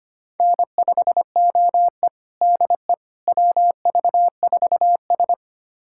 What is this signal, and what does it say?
Morse code, heard as a keyed tone, about 25 wpm, 700 Hz. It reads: N5OE DE WV4S
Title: USA Callsign DE Callsign - 4 Characters - 25wpm